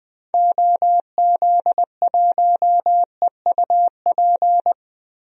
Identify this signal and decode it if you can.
Morse code, heard as a keyed tone, about 20 wpm, 700 Hz. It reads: OZ1EUP
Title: DXCC-100 Call Signs - 6 Characters - Repeated 2x - 20wpm